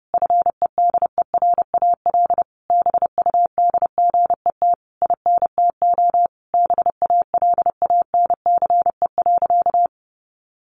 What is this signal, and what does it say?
Morse code, heard as a keyed tone, about 30 wpm, 700 Hz. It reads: FEDERAL BUDGET INTO BALANCE.
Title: Sentences from Top 1200 Words 30wpm